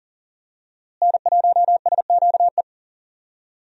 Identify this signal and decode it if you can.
Morse code, heard as a keyed tone, about 40 wpm, 700 Hz. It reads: N1SQE
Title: USA Call Signs - 5 Characters - Repeated 2x - 40wpm